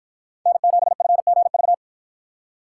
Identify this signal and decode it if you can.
Morse code, heard as a keyed tone, about 53 wpm, 700 Hz. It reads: N7FGV